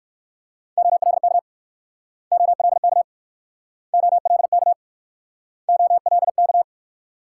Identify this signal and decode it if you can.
Morse code, heard as a keyed tone, about 62 wpm, 700 Hz. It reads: OLK OLK OLK OLK